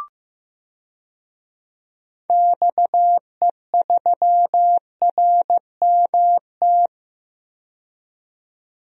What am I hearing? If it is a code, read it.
Morse code, heard as a keyed tone, about 15 wpm, 700 Hz. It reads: XE3RMT
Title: DXCC-100 Call Signs - 6 Characters - 15wpm